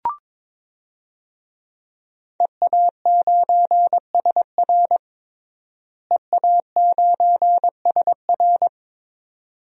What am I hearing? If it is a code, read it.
Morse code, heard as a keyed tone, about 22 wpm, 700 Hz. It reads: EA9SR EA9SR
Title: DXCC-100 Call Signs - 5 Characters - Repeated 2x - 22wpm